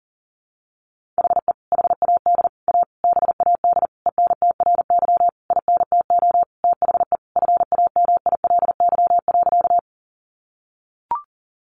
Morse code, heard as a keyed tone, about 40 wpm, 700 Hz. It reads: HE HAD A BAD ENTRY INTO THE FAMILY.
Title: Sentences from Top 1500 Words 40wpm